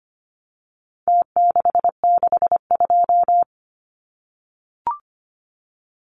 Morse code, heard as a keyed tone, about 25 wpm, 700 Hz. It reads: T662
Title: Numbers - Contest Style - 4-Digit 25wpm